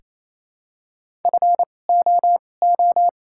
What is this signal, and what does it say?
Morse code, heard as a keyed tone, about 28 wpm, spaced as for 20 wpm, 700 Hz. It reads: FOO
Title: 2x Character Spacing - Top 500 Words - 28wpm